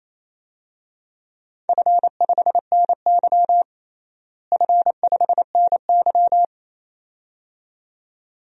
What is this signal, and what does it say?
Morse code, heard as a keyed tone, about 27 wpm, 700 Hz. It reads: F5NY F5NY
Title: DXCC-100 Call Signs - 4 Characters - Repeated 2x - 28wpm